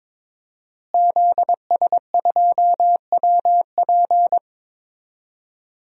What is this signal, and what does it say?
Morse code, heard as a keyed tone, about 22 wpm, 700 Hz. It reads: ZS2WP
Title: DXCC-100 Call Signs - 5 Characters - Repeated 3x - 22wpm